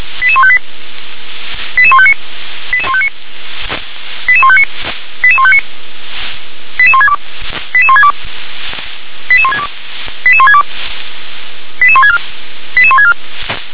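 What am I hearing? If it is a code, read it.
Five-tone selective call (ZVEI), heard as quick runs of five tones, tones of about 70 ms. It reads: 88158 88159 88159 88159 88159 88162 88162 88162 88162 88164 88164
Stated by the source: Five-tone ZVEI selective call received with an SDR.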